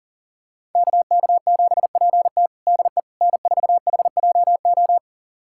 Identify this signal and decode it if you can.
Morse code, heard as a keyed tone, about 37 wpm, 700 Hz. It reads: KK7PT DE N4HJO
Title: USA Callsign DE Callsign - 5 Characters - 40wpm